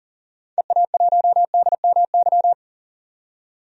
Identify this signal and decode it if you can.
Morse code, heard as a keyed tone, about 40 wpm, 700 Hz. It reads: EA1DMY
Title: DXCC-100 Call Signs - 6 Characters - 40wpm